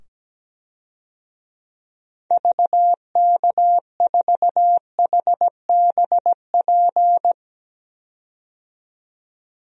Morse code, heard as a keyed tone, about 17 wpm, 700 Hz. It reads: VK4HBP
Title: DXCC-100 Call Signs - 6 Characters - 17wpm